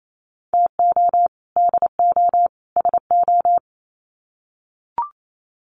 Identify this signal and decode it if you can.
Morse code, heard as a keyed tone, about 28 wpm, 700 Hz. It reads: TO DO SO
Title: Sets of 3 Words 28wpm